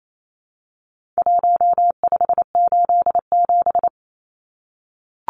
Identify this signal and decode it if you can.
Morse code, heard as a keyed tone, about 28 wpm, 700 Hz. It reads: 1587
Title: Numbers - 4-Digit 28wpm